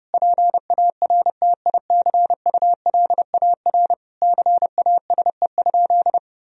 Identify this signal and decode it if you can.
Morse code, heard as a keyed tone, about 32 wpm, 700 Hz. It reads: PARTICULAR CASE?
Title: Sentences from Top 1000 Words - Review Part 4 - 30wpm